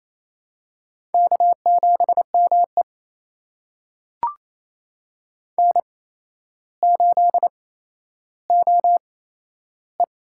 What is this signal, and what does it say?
Morse code, heard as a keyed tone, about 28 wpm, spaced as for 12 wpm, 700 Hz. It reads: K7ME N8OE
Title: USA Call Signs - 8x Character Spacing - 28wpm